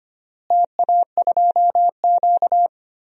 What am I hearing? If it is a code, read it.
Morse code, heard as a keyed tone, about 25 wpm, 700 Hz. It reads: TA2Q
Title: DXCC-100 Call Signs - 4 Characters - Repeated 2x - 25wpm